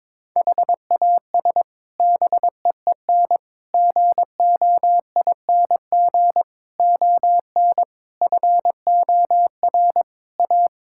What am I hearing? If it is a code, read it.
Morse code, heard as a keyed tone, about 22 wpm, 700 Hz. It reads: HAS BEEN GOING ON FOR A
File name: practice / 22wpm / Sets of 8 Words - 22wpm